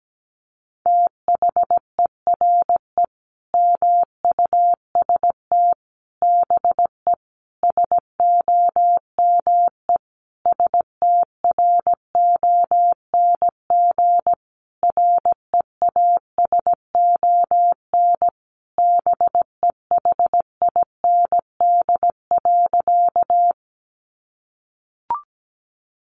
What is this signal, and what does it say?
Morse code, heard as a keyed tone, about 17 wpm, 700 Hz. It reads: THERE MUST BE SOME STRONG REASON BEHIND.
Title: Sentences from Top 700 Words 17wpm